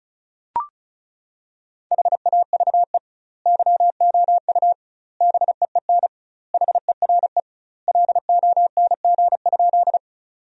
Morse code, heard as a keyed tone, about 36 wpm, 700 Hz. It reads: HAVE YOU BEEN HERE LONG?
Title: 2x Word-Spacing - Sentences from Top 200 Words - 35wpm